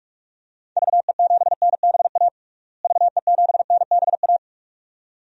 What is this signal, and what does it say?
Morse code, heard as a keyed tone, about 41 wpm, 700 Hz. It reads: VE7NBA VE7NBA
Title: DXCC-100 Call Signs - 6 Characters - Repeated 2x - 45wpm